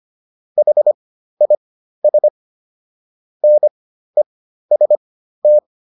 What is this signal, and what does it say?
Morse code, heard as a keyed tone, about 25 wpm, spaced as for 13 wpm, 600 Hz. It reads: HIS NEST